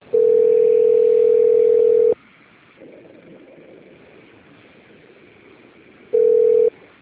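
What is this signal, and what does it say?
Ringback tone (North America), 440 + 480 Hz.
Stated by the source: Outbound phone call recorded on a telephone system at Piracicaba - SP.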